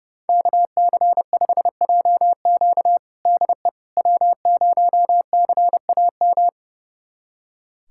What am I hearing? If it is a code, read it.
Morse code, heard as a keyed tone, about 29 wpm, 700 Hz. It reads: KC5JQ DE W0CAM
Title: USA Callsign DE Callsign - 5 Characters - 30wpm